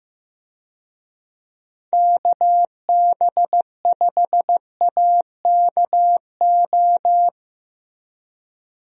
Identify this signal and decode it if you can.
Morse code, heard as a keyed tone, about 15 wpm, 700 Hz. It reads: KB5AKO
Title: USA Call Signs - 6 Characters - Repeated 2x - 15wpm